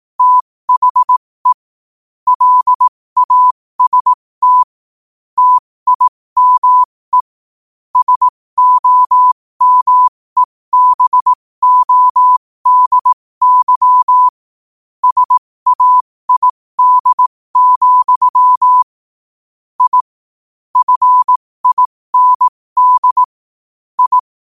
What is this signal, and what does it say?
Morse code, heard as a keyed tone, about 18 wpm, spaced as for 15 wpm, 1000 Hz. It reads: THE LAST TIME SOMEBODY SAID, I FIND I
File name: qotdfiles / 15wpm